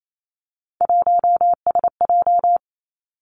Morse code, heard as a keyed tone, about 28 wpm, 690 Hz. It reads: 1SJ
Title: Three Challenging Letter-Number Combinations 28wpm